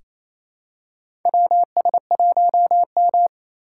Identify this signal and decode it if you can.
Morse code, heard as a keyed tone, about 28 wpm, 700 Hz. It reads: WS1M
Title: USA Call Signs - 2x Character Spacing - 28wpm